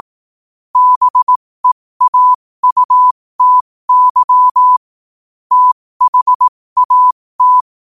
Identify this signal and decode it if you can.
Morse code, heard as a keyed tone, about 18 wpm, spaced as for 15 wpm, 1000 Hz. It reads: BEAUTY THAT
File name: qotdfiles / 15wpm